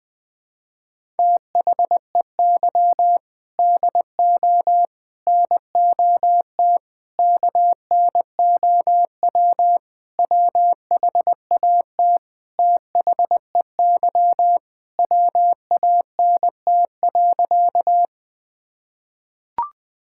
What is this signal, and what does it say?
Morse code, heard as a keyed tone, about 21 wpm, 700 Hz. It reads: THEY DO NOT KNOW WHAT THEY WANT.
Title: Sentences with 5-Word Phrases - 20wpm